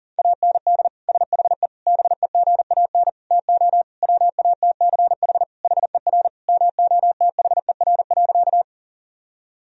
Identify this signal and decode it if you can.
Morse code, heard as a keyed tone, about 40 wpm, 700 Hz. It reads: AND SHE BEGAN TO WATCH HER MOTHER.
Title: Sentences from Top 1000 Words - Review Part 1 - 40wpm